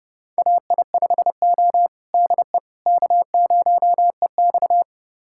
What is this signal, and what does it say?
Morse code, heard as a keyed tone, about 29 wpm, 700 Hz. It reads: AI5O DE K0EX
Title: USA Callsign DE Callsign - 4 Characters - 30wpm